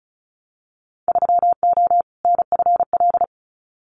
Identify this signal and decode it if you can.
Morse code, heard as a keyed tone, about 34 wpm, 700 Hz. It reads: 3O NFL
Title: Winter Field Day Exchanges 35wpm